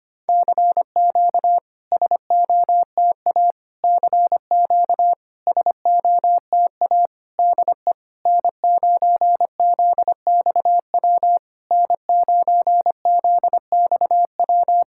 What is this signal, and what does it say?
Morse code, heard as a keyed tone, about 25 wpm, 700 Hz. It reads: CQ SOTA CQ SOTA DE N9ZXW N9ZXW
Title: SOTA QSO Practice 25wpm